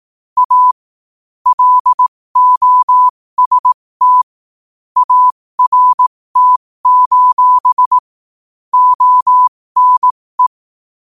Morse code, heard as a keyed tone, about 18 wpm, spaced as for 15 wpm, 1000 Hz. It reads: A LOST ART: ONE
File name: qotdfiles / 15wpm